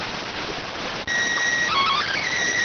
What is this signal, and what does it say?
SSTV (slow-scan television), PD 290.